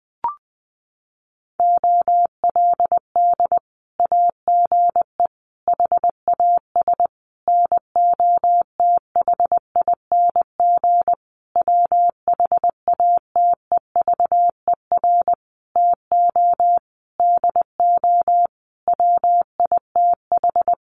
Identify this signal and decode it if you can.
Morse code, heard as a keyed tone, about 20 wpm, 700 Hz. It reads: OLD AGE HAS NOTHING WHATEVER TO DO WITH
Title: Sentences from Top 1100 Words 20wpm